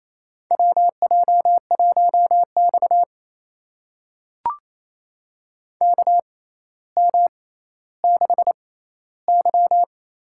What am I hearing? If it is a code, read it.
Morse code, heard as a keyed tone, about 28 wpm, spaced as for 14 wpm, 700 Hz. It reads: WJ1X KM6Y
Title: USA Call Signs - 6x Character Spacing - 28wpm